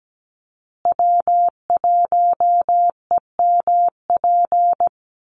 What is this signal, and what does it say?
Morse code, heard as a keyed tone, about 17 wpm, 700 Hz. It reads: W1EMP
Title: USA Call Signs Repeated 3x 17wpm